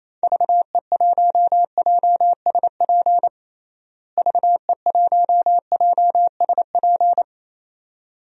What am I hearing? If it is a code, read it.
Morse code, heard as a keyed tone, about 28 wpm, 700 Hz. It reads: VE1JSP VE1JSP
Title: DXCC-100 Call Signs - 6 Characters - Repeated 2x - 28wpm